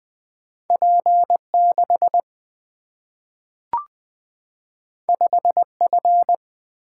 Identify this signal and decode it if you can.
Morse code, heard as a keyed tone, about 20 wpm, 700 Hz. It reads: P6 5F
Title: Single-Letter Single-Number 20wpm